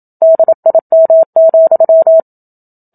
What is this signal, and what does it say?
Morse code, heard as a keyed tone, about 27 wpm, 650 Hz. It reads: DIM,
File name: Book Two / 01-Chapter 1 - 27 WPM